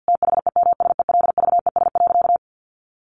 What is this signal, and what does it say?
Morse code, heard as a keyed tone, about 50 wpm, 700 Hz. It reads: THEMSELVES.
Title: Sentences from Top 1400 Words 50wpm